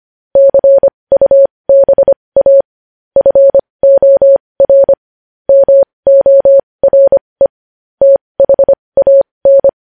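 Morse code, heard as a keyed tone, about 25 wpm, spaced as for 20 wpm, 550 Hz. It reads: CUBA FOR MORE THAN
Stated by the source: Morse code news headlines sent at 20 wpm.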